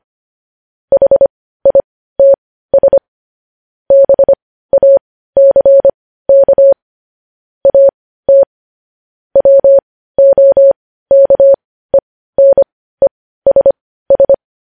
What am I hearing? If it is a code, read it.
Morse code, heard as a keyed tone, about 25 wpm, spaced as for 15 wpm, 550 Hz. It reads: HITS BACK AT WOKENESS
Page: news headlines in morse code at 15 wpm